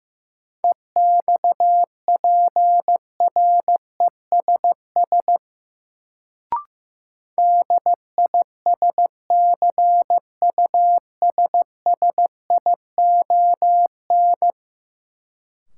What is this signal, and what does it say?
Morse code, heard as a keyed tone, about 15 wpm, 700 Hz. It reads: EXPRESS DISCUSSION
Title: Sentences from Top 1100 Words - New Word Drill - 15wpm